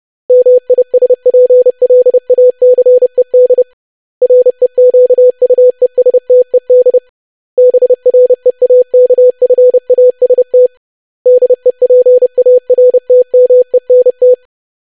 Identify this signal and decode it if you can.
Morse code, heard as a keyed tone, about 30 wpm, 500 Hz. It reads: MISPLACED REQUESTED BREAKFAST DEPARTMENT
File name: lesson / web / longwords-25wpm